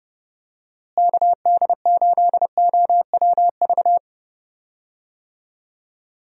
Morse code, heard as a keyed tone, about 30 wpm, 700 Hz. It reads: KD8OWV
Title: USA Call Signs - 6 Characters - Repeated 2x - 30wpm